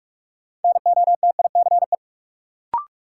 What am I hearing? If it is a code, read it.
Morse code, heard as a keyed tone, about 46 wpm, 700 Hz. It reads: NOTICE
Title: Sentences from Top 400 Words - New Word Drill - 45wpm